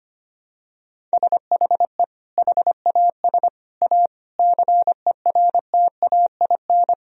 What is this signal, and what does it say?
Morse code, heard as a keyed tone, about 25 wpm, 700 Hz. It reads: SHE HAS A CERTAIN
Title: Sentences from Top 1000 Words - Review Part 4 - 25wpm